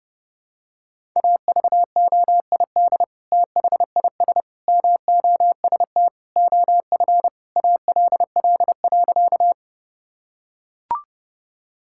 Morse code, heard as a keyed tone, about 30 wpm, 700 Hz. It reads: AVOID THIS MOST OF ALL.
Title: Sentences from Top 1000 Words - Review Part 6 - 30wpm